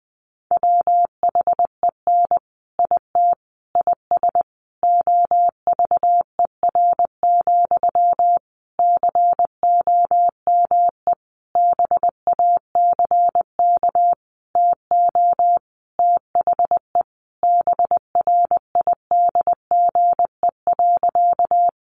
Morse code, heard as a keyed tone, about 20 wpm, 700 Hz. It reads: WHEN IT IS OVER, COME BACK TO THE BRIDGE.
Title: Sentences from Top 1200 Words 20wpm